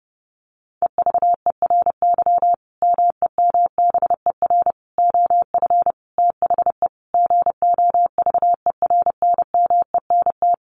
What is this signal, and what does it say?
Morse code, heard as a keyed tone, about 30 wpm, 700 Hz. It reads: EVERY MEMBER OF THE GOVERNMENT
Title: Sentences from Top 1400 Words 30wpm